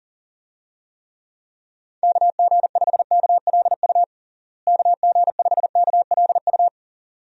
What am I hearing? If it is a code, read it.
Morse code, heard as a keyed tone, about 40 wpm, 700 Hz. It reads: KG5KLU KG5KLU
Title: USA Call Signs - 6 Characters - Repeated 2x - 40wpm